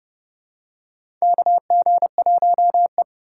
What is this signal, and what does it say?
Morse code, heard as a keyed tone, about 30 wpm, 700 Hz. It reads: KG1E
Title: USA Call Signs - 3x Character Spacing - 30wpm